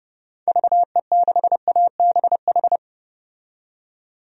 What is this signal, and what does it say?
Morse code, heard as a keyed tone, about 31 wpm, 700 Hz. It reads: VE6ABH
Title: DXCC-100 Call Signs - 6 Characters - 30wpm